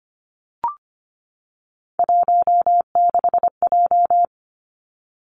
Morse code, heard as a keyed tone, about 25 wpm, 700 Hz. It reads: 16J
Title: Three Challenging Letter-Number Combinations 25wpm